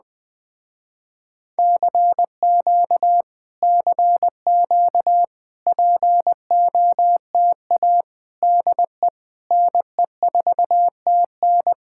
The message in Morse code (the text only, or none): CQ CQ POTA DE NE4TN